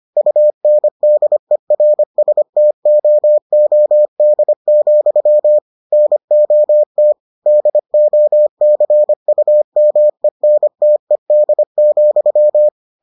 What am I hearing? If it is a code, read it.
Morse code, heard as a keyed tone, about 25 wpm, 600 Hz. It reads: UNDERSTOOD, NOT DOCUMENTED,